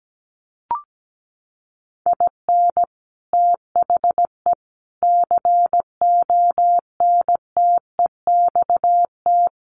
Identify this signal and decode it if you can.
Morse code, heard as a keyed tone, about 17 wpm, 700 Hz. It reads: IN THE CONTEXT
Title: Sets of 5 Words Encore 17wpm